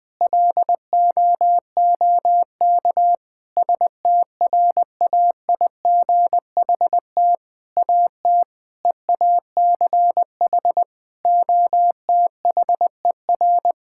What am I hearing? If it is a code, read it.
Morse code, heard as a keyed tone, about 20 wpm, 700 Hz. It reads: LOOK STRAIGHT AT EACH OTHER
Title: Sentences from Top 1000 Words - Review Part 6 - 20wpm